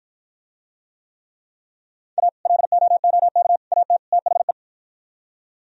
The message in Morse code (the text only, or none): A LOOK AT THE